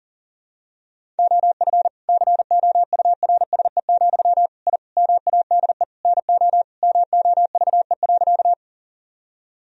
Morse code, heard as a keyed tone, about 40 wpm, 700 Hz. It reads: OF COURSE, I MADE NO MOVE.
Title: Sentences from Top 1000 Words - Review Part 1 - 40wpm